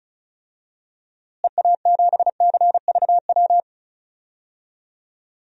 Morse code, heard as a keyed tone, about 34 wpm, 700 Hz. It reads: EA7CVW
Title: DXCC-100 Call Signs - 6 Characters - Repeated 3x - 35wpm